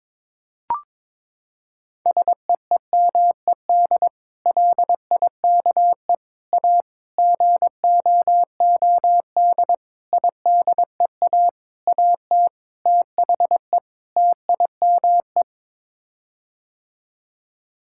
Morse code, heard as a keyed tone, about 22 wpm, 700 Hz. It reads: SEEMED LIKE A GOOD IDEA AT THE TIME
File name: practice / 22wpm / Sets of 8 Words - 22wpm